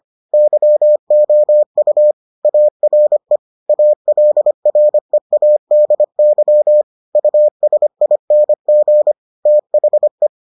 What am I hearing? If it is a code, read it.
Morse code, heard as a keyed tone, about 25 wpm, 600 Hz. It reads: YOU ARE ALREADY USING THE